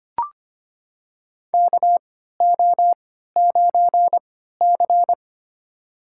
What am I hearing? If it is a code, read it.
Morse code, heard as a keyed tone, about 25 wpm, spaced as for 14 wpm, 700 Hz. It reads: KO9C
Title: USA Call Signs - 4 Characters - 3x Character Spacing - 25wpm